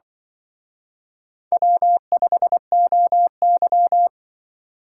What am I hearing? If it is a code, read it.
Morse code, heard as a keyed tone, about 24 wpm, 700 Hz. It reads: W5OY
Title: Speed Racing - USA 1x2 Call Signs 20wpm